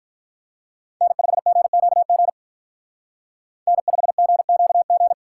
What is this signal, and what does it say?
Morse code, heard as a keyed tone, about 43 wpm, 700 Hz. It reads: N5GQG N5GQG
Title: Speed Racing - USA 1x3 Call Signs 35wpm